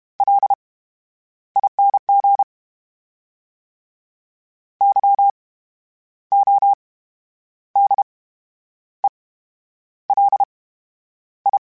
Morse code, heard as a keyed tone, about 32 wpm, 800 Hz. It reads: L I N G Y O D E L I
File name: Special INGs / Special INGs-mp3s / VerySlowToFaster Ing @30